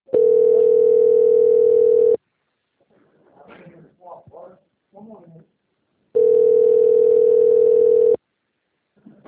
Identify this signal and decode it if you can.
Ringback tone (North America), 440 + 480 Hz, with speech.
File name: out- - - - - .14780